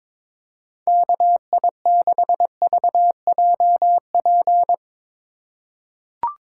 Morse code, heard as a keyed tone, about 21 wpm, 700 Hz. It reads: KI6VJP
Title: USA Call Signs - 4x Character Spacing - 22wpm